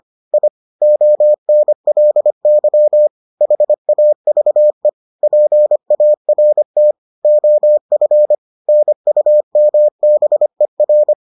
Morse code, heard as a keyed tone, about 25 wpm, 600 Hz. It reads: I ONLY HAVE PART OF NUMBER